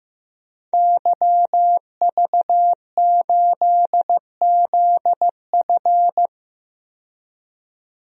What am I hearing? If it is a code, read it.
Morse code, heard as a keyed tone, about 15 wpm, 700 Hz. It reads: YV8ZF
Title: DXCC-100 Call Signs - 5 Characters - 15wpm